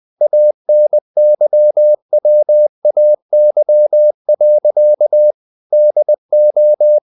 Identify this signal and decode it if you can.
Morse code, heard as a keyed tone, about 20 wpm, 600 Hz. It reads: ANYWAY. DO
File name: the old man and the sea part 2 20wpm